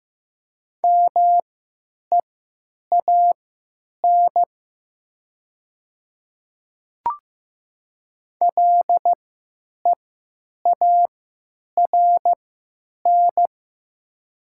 Morse code, heard as a keyed tone, about 15 wpm, spaced as for 9 wpm, 700 Hz. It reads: MEAN LEARN
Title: 3x Character Spacing - Top 300 Words - 15wpm